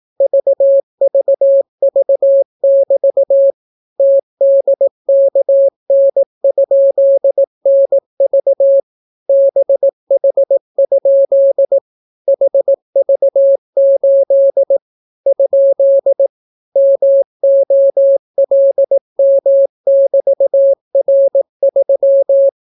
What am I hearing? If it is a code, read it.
Morse code, heard as a keyed tone, about 18 wpm, 550 Hz. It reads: VVV<BT> TDKN?NV BH? HV8 ? MOLM<BT>R3